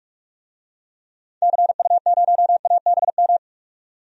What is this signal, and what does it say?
Morse code, heard as a keyed tone, about 47 wpm, 700 Hz. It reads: CU0ABM
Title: DXCC-100 Call Signs - 6 Characters - 45wpm